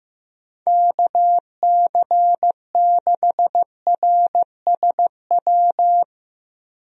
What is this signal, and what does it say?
Morse code, heard as a keyed tone, about 15 wpm, 700 Hz. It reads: KC6RSW
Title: USA Call Signs - 6 Characters - Repeated 3x - 15wpm